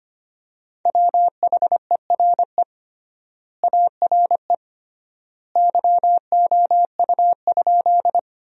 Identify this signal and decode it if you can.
Morse code, heard as a keyed tone, about 25 wpm, 700 Hz. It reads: WHERE ARE YOU?
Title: 3x Word-Spacing - Sentences from Top 200 Words - 25wpm